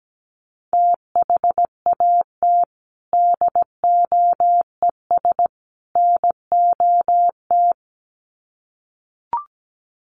Morse code, heard as a keyed tone, about 17 wpm, 700 Hz. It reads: THAT DOES NOT
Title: Sets of 3 Words 17wpm